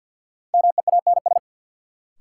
Morse code, heard as a keyed tone, about 52 wpm, 700 Hz. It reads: MEANS
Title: Sets of 5 Words - New Word Drill - 50wpm